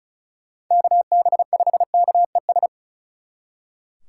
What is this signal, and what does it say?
Morse code, heard as a keyed tone, about 47 wpm, 700 Hz. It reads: KB5KES